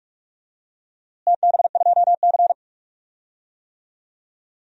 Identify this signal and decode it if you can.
Morse code, heard as a keyed tone, about 46 wpm, 700 Hz. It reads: TB2C
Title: DXCC-100 Call Signs - 4 Characters - Repeated 2x - 45wpm